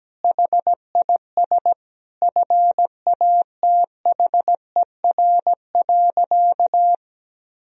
Morse code, heard as a keyed tone, about 17 wpm, 700 Hz. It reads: HIS FATHER.
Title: Sentences from Top 500 Words and Abbreviations 17wpm